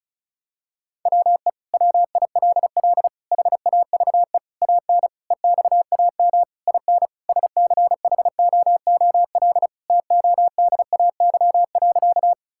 Morse code, heard as a keyed tone, about 36 wpm, 700 Hz. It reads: WE WILL HAVE AN EXAM IN SCHOOL TODAY.
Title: Sentences from Top 1000 Words - Review Part 5 - 35wpm